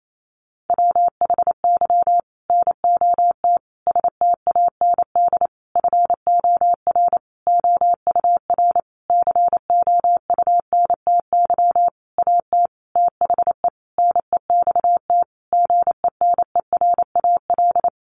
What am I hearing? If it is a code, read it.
Morse code, heard as a keyed tone, about 28 wpm, 700 Hz. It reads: WHY NOT STAND FOR OUR COUNTY AT THE NEXT GENERAL
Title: Sentences from Top 1500 Words 28wpm